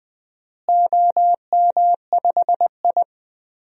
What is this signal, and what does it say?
Morse code, heard as a keyed tone, about 20 wpm, 700 Hz. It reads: OM5I
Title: DXCC-100 Call Signs - 4 Characters - Repeated 3x - 20wpm